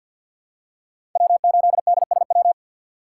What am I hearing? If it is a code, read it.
Morse code, heard as a keyed tone, about 51 wpm, 700 Hz. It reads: W8DSW
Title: USA Call Signs - 5 Characters - 4x Character Spacing - 50wpm